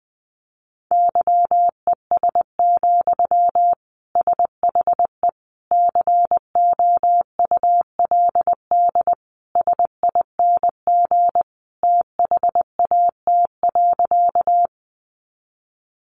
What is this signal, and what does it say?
Morse code, heard as a keyed tone, about 20 wpm, 700 Hz. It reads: YES, SHE COULD SING THAT.
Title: Sentences from Top 500 Words 20wpm